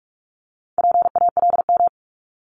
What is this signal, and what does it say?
Morse code, heard as a keyed tone, about 46 wpm, 700 Hz. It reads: PALM
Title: Multi-letter Phonemes 45wpm